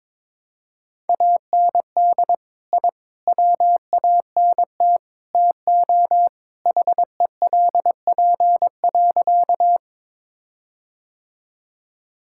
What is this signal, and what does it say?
Morse code, heard as a keyed tone, about 22 wpm, 700 Hz. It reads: AND I WANT TO HELP.
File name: practice / 22wpm / Sentences with 4-Word Phrases - 22wpm